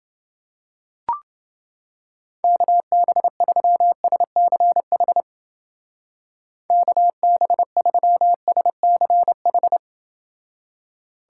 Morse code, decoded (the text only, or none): KB3SCH KB3SCH